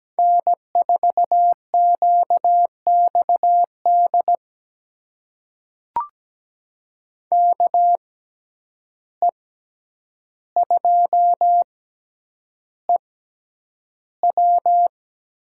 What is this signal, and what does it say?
Morse code, heard as a keyed tone, about 17 wpm, spaced as for 9 wpm, 700 Hz. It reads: N4QXD KE2EW
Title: USA Call Signs - 5 Characters - 6x Character Spacing - 17wpm